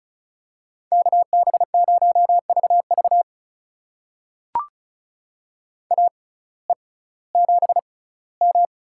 Morse code, heard as a keyed tone, about 34 wpm, spaced as for 19 wpm, 700 Hz. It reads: KB0VV AE7M
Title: USA Call Signs - 5 Characters - 6x Character Spacing - 35wpm